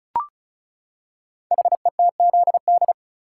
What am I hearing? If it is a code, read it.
Morse code, heard as a keyed tone, about 36 wpm, 690 Hz. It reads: HETZD
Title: Code Groups - 5 Letters - 35wpm